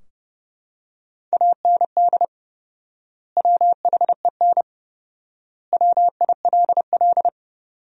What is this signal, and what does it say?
Morse code, heard as a keyed tone, about 30 wpm, 700 Hz. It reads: AND WHEN WILL
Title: 4x Word-Spacing - Sentences from Top 100 Words - 30wpm